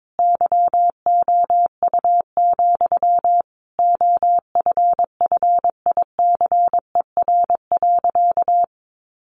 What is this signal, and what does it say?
Morse code, heard as a keyed tone, about 22 wpm, 700 Hz. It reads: YOU, OFFICER.
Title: Sentences from Top 1200 Words 22wpm